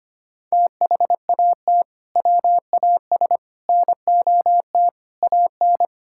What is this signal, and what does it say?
Morse code, heard as a keyed tone, about 25 wpm, 700 Hz. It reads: THAT WAS NOT AN
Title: Sentences from Top 1000 Words - Review Part 5 - 25wpm